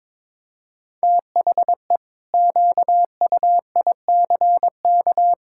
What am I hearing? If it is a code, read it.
Morse code, heard as a keyed tone, about 22 wpm, 700 Hz. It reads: THE QUICK
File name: practice / 22wpm / Sentences with 8-Word Phrases - 22wpm